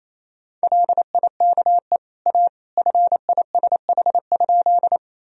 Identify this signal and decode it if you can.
Morse code, heard as a keyed tone, about 28 wpm, 700 Hz. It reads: LIKE A FISH?
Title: Sentences from Top 1000 Words - Review Part 1 - 28wpm